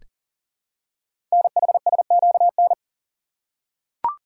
Morse code, heard as a keyed tone, about 40 wpm, 700 Hz. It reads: NHSQN
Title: Code Groups - 5 Letter:Numbers - 40wpm